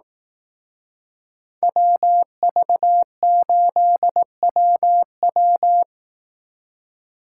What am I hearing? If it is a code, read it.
Morse code, heard as a keyed tone, about 18 wpm, 700 Hz. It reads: WV8WW
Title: Speed Racing - USA 2x2 Call Signs 15wpm